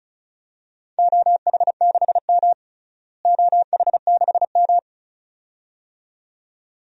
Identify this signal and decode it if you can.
Morse code, heard as a keyed tone, about 34 wpm, 700 Hz. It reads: OH6M OH6M
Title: DXCC-100 Call Signs - 4 Characters - Repeated 2x - 35wpm